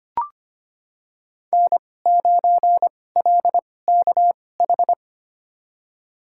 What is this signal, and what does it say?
Morse code, heard as a keyed tone, about 25 wpm, spaced as for 18 wpm, 700 Hz. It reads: N9LKH
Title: USA Call Signs - 5 Characters - 2x Character Spacing - 25wpm